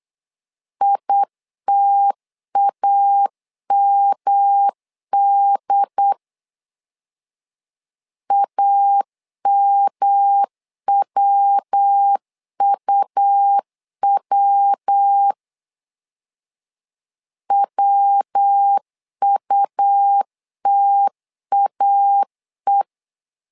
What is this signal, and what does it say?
Morse code, heard as a keyed tone, about 8 wpm, 800 Hz. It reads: ITAMD AMWUW WUTAE
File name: LEZIONI / 3 MWRO / 3 MW 8wpm 2501